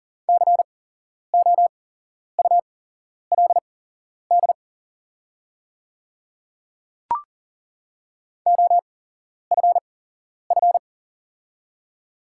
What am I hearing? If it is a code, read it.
Morse code, heard as a keyed tone, about 39 wpm, spaced as for 11 wpm, 700 Hz. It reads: COULD OFF
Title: 8x Character Spacing - Top 300 Words - 40wpm